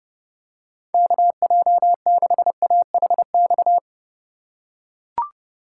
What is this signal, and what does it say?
Morse code, heard as a keyed tone, about 31 wpm, 700 Hz. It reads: KJ6AHX